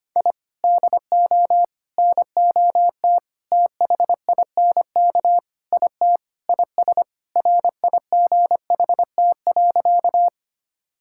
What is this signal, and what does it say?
Morse code, heard as a keyed tone, about 25 wpm, 700 Hz. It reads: I DO NOT THINK IT IS RIGHT.
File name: practice / 25wpm / Sentences with 4-Word Phrases - 25wpm